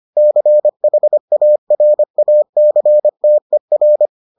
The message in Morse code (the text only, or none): CHARACTER